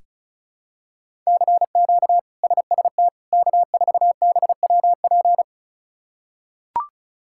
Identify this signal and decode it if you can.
Morse code, heard as a keyed tone, about 36 wpm, 700 Hz. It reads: CQ SST K4BWP
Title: K1USN SST QSO Practice - 35wpm